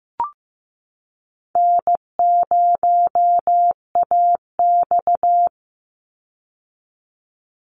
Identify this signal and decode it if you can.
Morse code, heard as a keyed tone, about 15 wpm, 700 Hz. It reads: N0AX
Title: Call Signs Repeated 3x - 15wpm